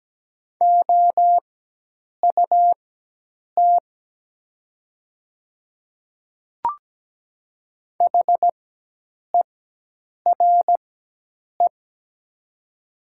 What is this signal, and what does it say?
Morse code, heard as a keyed tone, about 17 wpm, spaced as for 8 wpm, 700 Hz. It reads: OUT HERE
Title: 4x Character Spacing - Top 300 Words - 17wpm